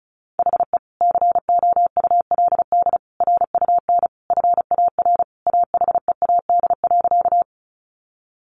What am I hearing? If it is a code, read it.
Morse code, heard as a keyed tone, about 34 wpm, 700 Hz. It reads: HE COULD RUN FAR AHEAD.
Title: Sentences from Top 1100 Words 35wpm